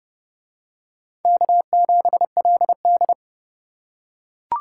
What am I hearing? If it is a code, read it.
Morse code, heard as a keyed tone, about 29 wpm, 700 Hz. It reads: K7LD